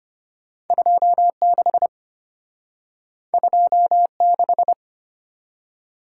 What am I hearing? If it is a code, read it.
Morse code, heard as a keyed tone, about 32 wpm, 700 Hz. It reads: 26 26